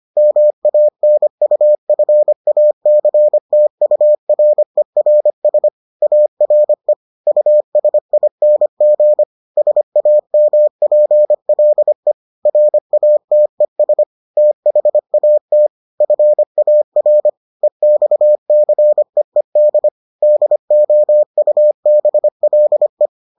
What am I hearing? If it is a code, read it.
Morse code, heard as a keyed tone, about 25 wpm, 600 Hz. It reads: MANUFACTURERS ARE USING SAMPLE RATES THAT FAR EXCEED DOUBLE